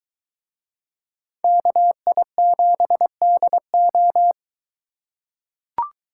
Morse code, heard as a keyed tone, about 22 wpm, 700 Hz. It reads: KI7DO